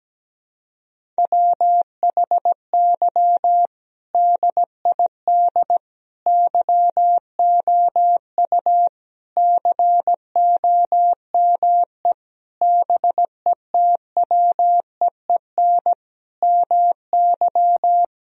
Morse code, heard as a keyed tone, about 17 wpm, 700 Hz. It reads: WHY DID YOU COME BETWEEN MY
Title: Sentences from Top 1000 Words - Review Part 1 - 17wpm